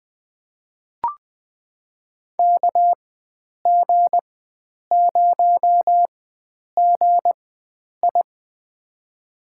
Morse code, heard as a keyed tone, about 21 wpm, spaced as for 9 wpm, 700 Hz. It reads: KG0GI
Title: USA Call Signs - 5 Characters - 4x Character Spacing - 20wpm